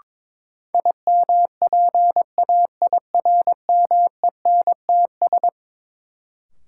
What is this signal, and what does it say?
Morse code, heard as a keyed tone, about 22 wpm, 700 Hz. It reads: IMPAIRMENTS
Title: Sets of 5 Words Encore - New Word Drill - 22wpm